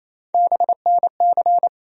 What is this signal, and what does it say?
Morse code, heard as a keyed tone, about 28 wpm, 700 Hz. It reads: BNC
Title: USA Call Sign Postfixes - 3 Letters 28wpm